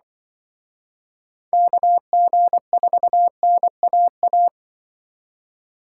Morse code, heard as a keyed tone, about 24 wpm, 700 Hz. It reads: KG4NAA
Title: Speed Racing - USA 2x3 Call Signs 20wpm